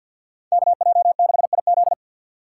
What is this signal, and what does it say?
Morse code, heard as a keyed tone, about 51 wpm, 700 Hz. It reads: KJ6IZ
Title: USA Call Signs - 3x Character Spacing - 50wpm